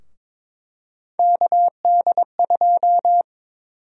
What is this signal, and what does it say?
Morse code, heard as a keyed tone, about 22 wpm, 700 Hz. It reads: KD2
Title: USA Call Sign Prefixes - 22wpm